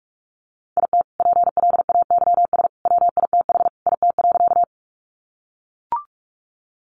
Morse code, heard as a keyed tone, about 44 wpm, 700 Hz. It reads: IT PLAYS WITH IT.
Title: Additional Easy Phrases 45wpm